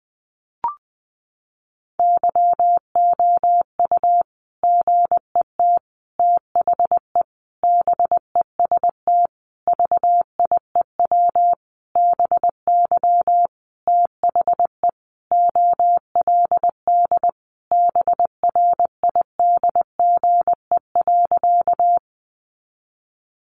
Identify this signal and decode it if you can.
Morse code, heard as a keyed tone, about 20 wpm, 700 Hz. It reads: YOU GET THE BEST VIEW BY THE OLD BRIDGE.
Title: Sentences from Top 1200 Words 20wpm